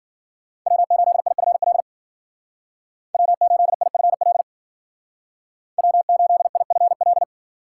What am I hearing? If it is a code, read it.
Morse code, heard as a keyed tone, about 49 wpm, 700 Hz. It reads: W8IFL W8IFL W8IFL
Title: Speed Racing - USA 1x3 Call Signs 40wpm